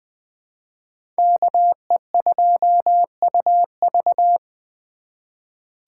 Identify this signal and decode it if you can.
Morse code, heard as a keyed tone, about 20 wpm, 700 Hz. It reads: KE2UV
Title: USA Call Signs - 5 Characters - 4x Character Spacing - 20wpm